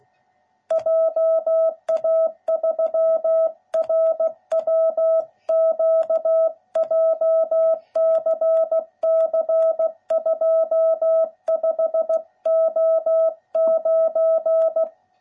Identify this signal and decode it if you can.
Morse code, heard as a keyed tone, about 17 wpm, 660 Hz. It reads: JA3RWQ JCC 25O9